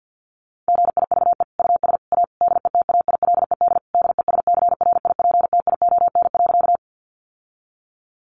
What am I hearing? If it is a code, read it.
Morse code, heard as a keyed tone, about 50 wpm, 700 Hz. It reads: GIVE US A DETAILED DESCRIPTION.
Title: Sentences from Top 1500 Words 50wpm